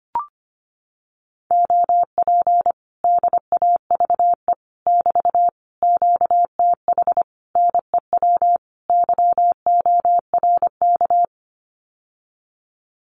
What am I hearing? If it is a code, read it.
Morse code, heard as a keyed tone, about 25 wpm, 700 Hz. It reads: OP DAVE <BT> QTH NEW YORK
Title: Common QSO Phrases Encore 25wpm